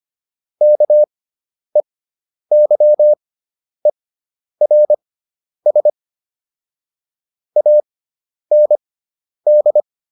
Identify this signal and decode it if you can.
Morse code, heard as a keyed tone, about 25 wpm, spaced as for 10 wpm, 600 Hz. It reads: KEYERS AND